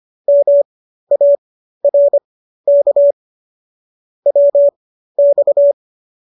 Morse code, heard as a keyed tone, about 25 wpm, spaced as for 13 wpm, 570 Hz. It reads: MARK WX